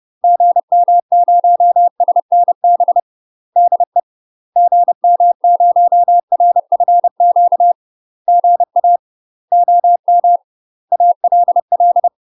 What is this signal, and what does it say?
Morse code, heard as a keyed tone, about 30 wpm, 700 Hz. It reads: GM0SNB DE GM0RFQ GA OM ALL
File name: UK MASTER QSOs TRAINING / 30WPM / UK MASTER QSOs 0041